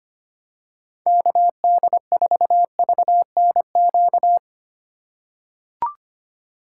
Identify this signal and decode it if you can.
Morse code, heard as a keyed tone, about 25 wpm, 700 Hz. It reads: KD4VNQ
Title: USA Call Signs - 6 Characters - 4x Character Spacing - 25wpm